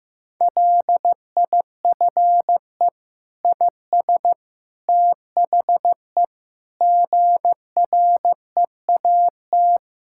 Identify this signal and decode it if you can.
Morse code, heard as a keyed tone, about 15 wpm, 700 Hz. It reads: LIFE IS THE GREAT
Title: Sentences from Top 1000 Words - Review Part 6 - 15wpm